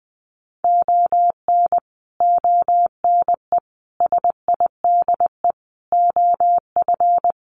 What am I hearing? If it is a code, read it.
Morse code, heard as a keyed tone, about 20 wpm, 700 Hz. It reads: ON ONE SIDE OF
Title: Sets of 5 Words 20wpm